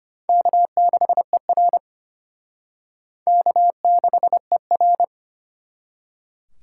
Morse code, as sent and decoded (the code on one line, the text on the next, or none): -.- -.... .-. / -.- -.... .-.
K6ER K6ER